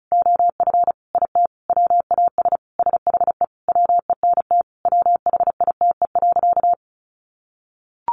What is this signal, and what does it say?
Morse code, heard as a keyed tone, about 34 wpm, 700 Hz. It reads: OF IT WAS SHE WENT WHITE.
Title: Sentences from Top 1200 Words 35wpm